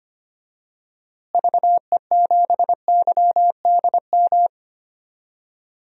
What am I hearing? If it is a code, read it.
Morse code, heard as a keyed tone, about 25 wpm, 700 Hz. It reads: VE7YDM
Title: DXCC-100 Call Signs - 6 Characters - Repeated 2x - 25wpm